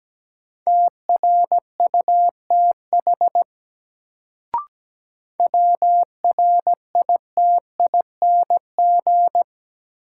Morse code, heard as a keyed tone, about 17 wpm, 700 Hz. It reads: TRUTH WRITING
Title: Sentences from Top 800 Words - New Word Drill - 17wpm